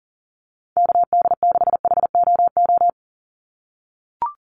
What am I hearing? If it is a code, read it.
Morse code, heard as a keyed tone, about 40 wpm, 690 Hz. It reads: KD6HOO
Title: USA Call Signs 40wpm